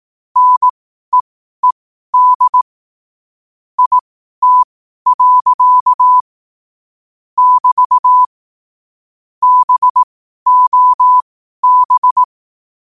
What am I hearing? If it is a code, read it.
Morse code, heard as a keyed tone, about 18 wpm, spaced as for 12 wpm, 1000 Hz. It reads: NEED IT. <BT> BOB
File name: qotdfiles / 12wpm